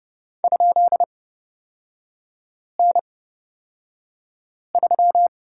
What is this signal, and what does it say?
Morse code, heard as a keyed tone, about 30 wpm, spaced as for 5 wpm, 700 Hz. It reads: ?N3
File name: Lesson 37e - Letter x - Three Characters 30wpm